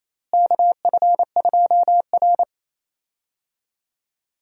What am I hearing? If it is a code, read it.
Morse code, heard as a keyed tone, about 28 wpm, 700 Hz. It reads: KF2R
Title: USA Call Signs - 4 Characters - Repeated 2x - 28wpm